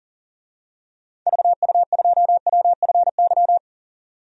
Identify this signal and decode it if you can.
Morse code, heard as a keyed tone, about 40 wpm, 700 Hz. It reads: VU2WFY
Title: DXCC-100 Call Signs - 6 Characters - 40wpm